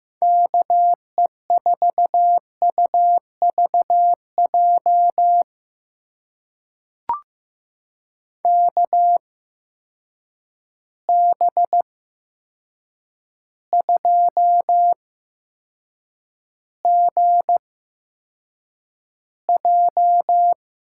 Morse code, heard as a keyed tone, about 15 wpm, spaced as for 7 wpm, 700 Hz. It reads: KE4UVJ KB2GJ
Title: USA Call Signs - 6 Characters - 8x Character Spacing - 15wpm